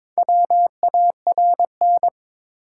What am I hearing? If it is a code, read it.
Morse code, heard as a keyed tone, about 22 wpm, 700 Hz. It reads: WARN
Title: Four Letter Words - 22wpm